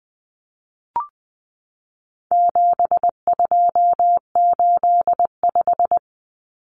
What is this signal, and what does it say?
Morse code, heard as a keyed tone, about 20 wpm, 700 Hz. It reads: 7285
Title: Common Cut Numbers - 4-Digit 20wpm